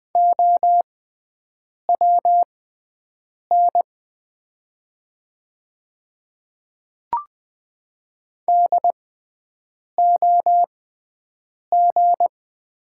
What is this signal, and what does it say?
Morse code, heard as a keyed tone, about 20 wpm, spaced as for 7 wpm, 700 Hz. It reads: OWN DOG
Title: 6x Character Spacing - Top 500 Words - 20wpm